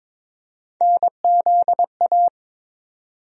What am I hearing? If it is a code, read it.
Morse code, heard as a keyed tone, about 22 wpm, 700 Hz. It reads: NZA
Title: USA Call Sign Postfixes - 3 Letters 22wpm